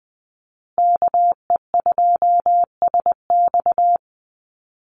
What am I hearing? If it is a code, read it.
Morse code, heard as a keyed tone, about 20 wpm, 700 Hz. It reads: KE2SX
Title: Call Signs Repeated 2x - 20wpm